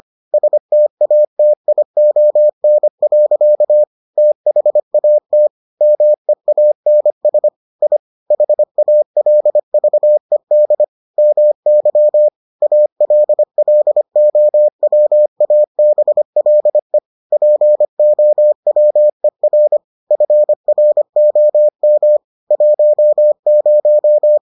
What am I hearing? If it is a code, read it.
Morse code, heard as a keyed tone, about 25 wpm, 600 Hz. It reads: STATION. THAT MEANS I HALVED MY ALLOWABLE POWER FROM 10